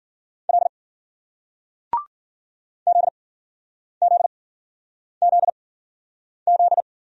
Morse code, heard as a keyed tone, about 53 wpm, spaced as for 9 wpm, 700 Hz. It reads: L ZZZZ